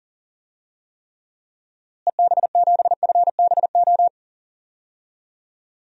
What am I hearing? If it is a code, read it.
Morse code, heard as a keyed tone, about 37 wpm, 700 Hz. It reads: EB7FBO
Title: DXCC-100 Call Signs - 6 Characters - Repeated 2x - 40wpm